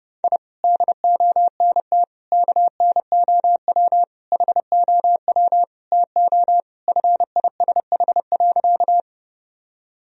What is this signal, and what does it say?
Morse code, heard as a keyed tone, about 30 wpm, 700 Hz. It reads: I DONT KNOW HOW TO FISH.
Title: Sentences from Top 1000 Words - Review Part 1 - 30wpm